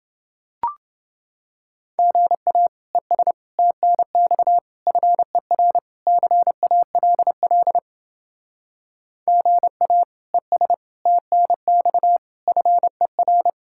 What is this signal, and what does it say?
Morse code, heard as a keyed tone, about 29 wpm, 700 Hz. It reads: GA ES TNX FER CALL GA ES TNX FER